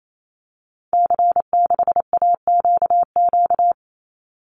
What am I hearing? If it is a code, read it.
Morse code, heard as a keyed tone, about 28 wpm, 700 Hz. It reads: C6AQQ
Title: Call Signs Repeated 2x - 28wpm